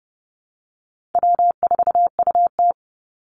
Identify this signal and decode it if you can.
Morse code, heard as a keyed tone, about 30 wpm, 700 Hz. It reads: W4UT
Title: Call Signs Repeated 3x - 30wpm